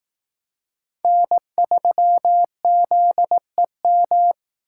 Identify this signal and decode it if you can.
Morse code, heard as a keyed tone, about 18 wpm, 700 Hz. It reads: N3ZEM
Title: Speed Racing - USA 1x3 Call Signs 15wpm